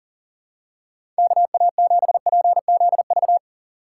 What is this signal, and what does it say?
Morse code, heard as a keyed tone, about 40 wpm, 700 Hz. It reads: KA7PZV
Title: USA Call Signs - 6 Characters - 2x Character Spacing - 40wpm